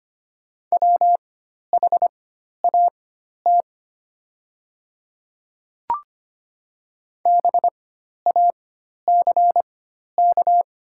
Morse code, heard as a keyed tone, about 25 wpm, spaced as for 12 wpm, 700 Hz. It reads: WHAT BACK
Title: 4x Character Spacing - Top 200 Words - 25wpm